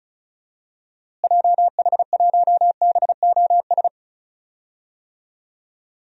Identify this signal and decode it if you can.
Morse code, heard as a keyed tone, about 36 wpm, 700 Hz. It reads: JH1BOS
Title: DXCC-100 Call Signs - 6 Characters - Repeated 2x - 35wpm